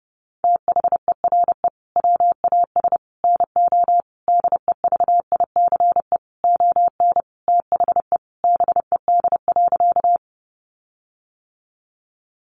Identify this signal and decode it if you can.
Morse code, heard as a keyed tone, about 30 wpm, 700 Hz. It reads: THERE WAS NO DEVICE ON THE BED.
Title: Sentences from Top 1500 Words 30wpm